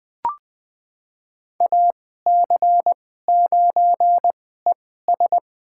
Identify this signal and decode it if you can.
Morse code, heard as a keyed tone, about 21 wpm, spaced as for 15 wpm, 700 Hz. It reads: AC9ES
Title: USA Call Signs - 2x Character Spacing - 20wpm